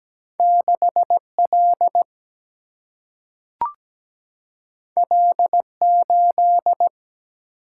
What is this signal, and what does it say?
Morse code, heard as a keyed tone, about 17 wpm, 700 Hz. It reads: 6L L8
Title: Single-Letter Single-Number 17wpm